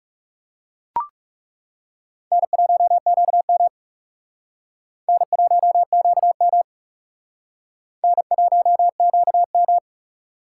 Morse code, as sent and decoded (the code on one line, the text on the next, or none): -. .---- --.- -- / -. .---- --.- -- / -. .---- --.- --
N1QM N1QM N1QM